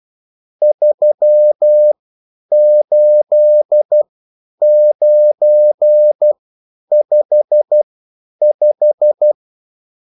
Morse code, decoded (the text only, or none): 38955